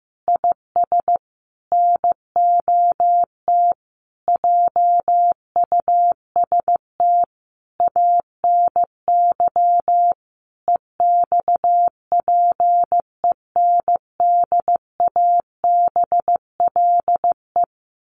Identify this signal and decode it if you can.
Morse code, heard as a keyed tone, about 15 wpm, 700 Hz. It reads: IS NOT JUST ANY EXPENDABLE
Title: A Sound Of Thunder 15wpm